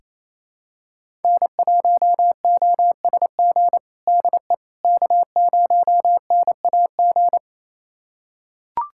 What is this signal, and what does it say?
Morse code, heard as a keyed tone, about 28 wpm, 700 Hz. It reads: N1OSG DE K0NAG
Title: USA Callsign DE Callsign - 5 Characters - 28wpm